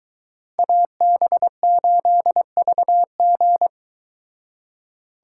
Morse code, heard as a keyed tone, about 22 wpm, 700 Hz. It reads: AB8VG